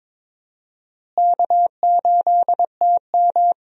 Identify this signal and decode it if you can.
Morse code, heard as a keyed tone, about 22 wpm, 700 Hz. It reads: K8TM
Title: USA Call Signs - 4 Characters - Repeated 2x - 22wpm